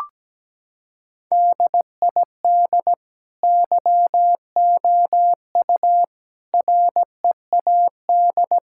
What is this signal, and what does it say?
Morse code, heard as a keyed tone, about 17 wpm, 700 Hz. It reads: DID YOU READ
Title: Sentences from Top 1000 Words - Review Part 5 - 17wpm